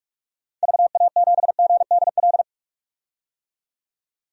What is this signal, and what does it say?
Morse code, heard as a keyed tone, about 42 wpm, 700 Hz. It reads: VA7GDL
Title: DXCC-100 Call Signs - 6 Characters - 45wpm